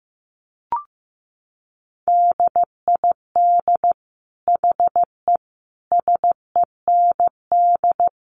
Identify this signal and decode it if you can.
Morse code, heard as a keyed tone, about 15 wpm, 700 Hz. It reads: DID HE SEND
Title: Sentences from Top 1500 Words 15wpm